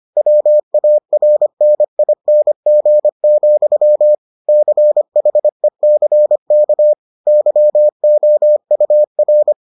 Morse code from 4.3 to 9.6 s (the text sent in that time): CHECK YOUR